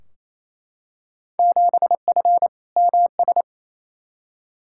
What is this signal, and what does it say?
Morse code, heard as a keyed tone, about 28 wpm, 700 Hz. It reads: 7F MS
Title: Field Day Exchanges Encore - 28wpm